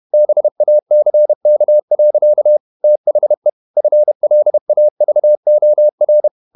Morse code, heard as a keyed tone, about 31 wpm, 600 Hz. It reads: BACK. THE FLAVOR
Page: Never 31wpm | CW med Gnister